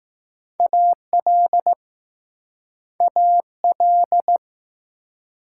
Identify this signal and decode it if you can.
Morse code, heard as a keyed tone, about 20 wpm, 700 Hz. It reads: AL AL